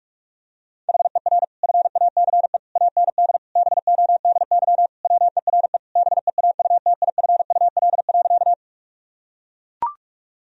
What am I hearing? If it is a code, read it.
Morse code, heard as a keyed tone, about 41 wpm, 700 Hz. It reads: HER FACE AND BODY WERE BEAUTIFUL.
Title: Sentences from Top 1000 Words - Review Part 2 - 45wpm